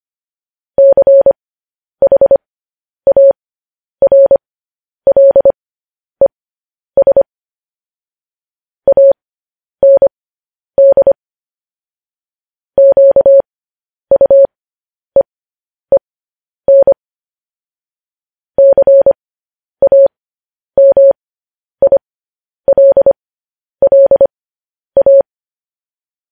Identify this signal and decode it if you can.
Morse code, heard as a keyed tone, about 25 wpm, spaced as for 10 wpm, 550 Hz. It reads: CHARLES AND QUEEN CAMILLA
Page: News Headlines in Morse Code at 10 WPM